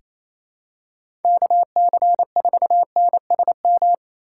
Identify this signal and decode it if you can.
Morse code, heard as a keyed tone, about 28 wpm, 700 Hz. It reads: KC4NSM
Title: USA Call Signs - 6 Characters - 4x Character Spacing - 28wpm